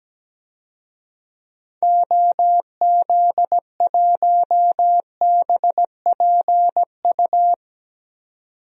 Morse code, heard as a keyed tone, about 17 wpm, 700 Hz. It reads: OZ1BPU
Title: DXCC-100 Call Signs - 6 Characters - 17wpm